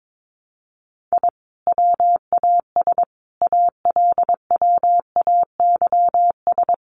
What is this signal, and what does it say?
Morse code, heard as a keyed tone, about 22 wpm, 700 Hz. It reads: I WAS ALWAYS
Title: A Sound Of Thunder 22wpm